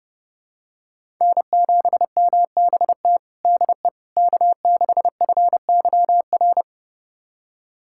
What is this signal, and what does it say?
Morse code, heard as a keyed tone, about 31 wpm, 700 Hz. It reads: N7MBT DE K6FYR
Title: USA Callsign DE Callsign - 5 Characters - 30wpm